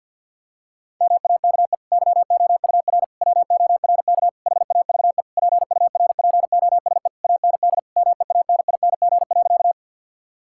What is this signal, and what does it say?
Morse code, heard as a keyed tone, about 50 wpm, 700 Hz. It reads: MAKE YOUR WORK HAVE PURPOSE AND MEANING.
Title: Sentences from Top 1000 Words - Review Part 6 - 50wpm